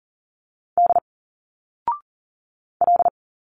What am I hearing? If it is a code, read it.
Morse code, heard as a keyed tone, about 39 wpm, 700 Hz. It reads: D L
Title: Single Letters 40wpm